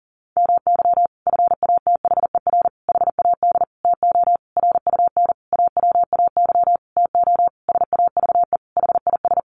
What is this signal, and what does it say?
Morse code, heard as a keyed tone, about 37 wpm, 700 Hz. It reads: MY FATHER HAD TO RUN AWAY TO SAVE HIS
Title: Sentences from Top 1100 Words 40wpm